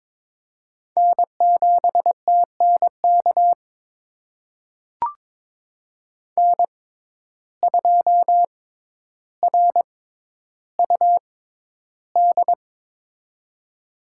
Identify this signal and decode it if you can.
Morse code, heard as a keyed tone, about 22 wpm, spaced as for 11 wpm, 700 Hz. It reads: N7TNK N2RUD
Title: USA Call Signs - 5 Characters - 6x Character Spacing - 22wpm